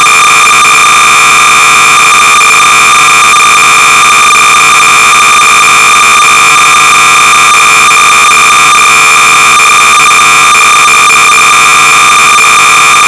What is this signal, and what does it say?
Cassette data (Kansas City type).
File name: sorcerer / 300baud / Wilderness-(Dick Smith Electronics)-BASIC-Sorcerer-300baud